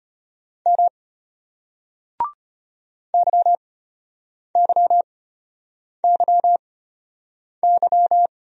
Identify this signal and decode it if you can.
Morse code, heard as a keyed tone, about 33 wpm, spaced as for 8 wpm, 700 Hz. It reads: M YYYY